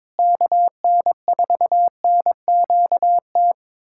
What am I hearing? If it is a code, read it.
Morse code, heard as a keyed tone, about 22 wpm, 700 Hz. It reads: KN4NQT
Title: USA Callsign DE Callsign 22wpm